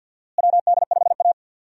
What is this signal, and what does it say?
Morse code, heard as a keyed tone, about 48 wpm, 700 Hz. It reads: WD5A
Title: USA Call Signs - 4 Characters - Repeated 3x - 50wpm